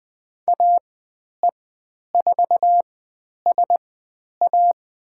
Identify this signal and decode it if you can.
Morse code, heard as a keyed tone, about 20 wpm, spaced as for 10 wpm, 700 Hz. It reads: AE4SA
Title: 109 Lesson 18g - Letter m - Call Signs 20wpm